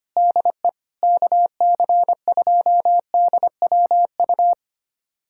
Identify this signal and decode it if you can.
Morse code, heard as a keyed tone, about 25 wpm, 700 Hz. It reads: DE KC2DWU
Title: USA Callsign DE Callsign - 6 Characters - 25wpm